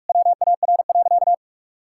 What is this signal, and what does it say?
Morse code, heard as a keyed tone, about 46 wpm, 700 Hz. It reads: WAR.
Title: Sentences from Top 1000 Words - Review Part 6 - 45wpm